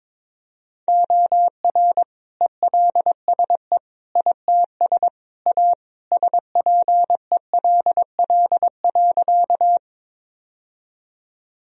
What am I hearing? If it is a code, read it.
Morse code, heard as a keyed tone, about 21 wpm, 700 Hz. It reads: OR ELSE ITS A SPELL.
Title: Sentences from Top 1000 Words - Review Part 1 - 22wpm